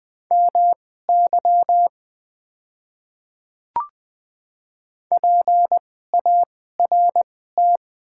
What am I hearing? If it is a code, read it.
Morse code, heard as a keyed tone, about 20 wpm, spaced as for 15 wpm, 700 Hz. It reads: MY PART
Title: 2x Character Spacing - Top 300 Words - 20wpm